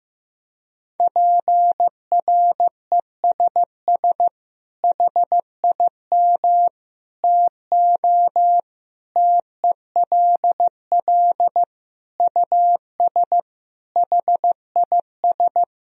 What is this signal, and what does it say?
Morse code, heard as a keyed tone, about 15 wpm, 700 Hz. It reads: PRESS HIM TO TELL US HIS
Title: Sentences from Top 1000 Words - Review Part 1 - 15wpm